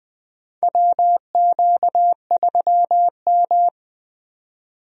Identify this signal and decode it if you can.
Morse code, heard as a keyed tone, about 20 wpm, 700 Hz. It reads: WQ3M
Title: USA Call Signs - 4 Characters - 20wpm